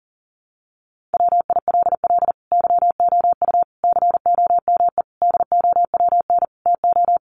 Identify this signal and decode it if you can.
Morse code, heard as a keyed tone, about 40 wpm, 700 Hz. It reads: WILL YOU COME DOWN TO
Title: Sentences from Top 1500 Words 40wpm